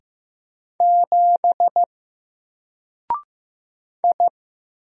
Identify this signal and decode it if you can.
Morse code, heard as a keyed tone, about 15 wpm, 700 Hz. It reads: 7 I
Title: Single Letter-Number 15wpm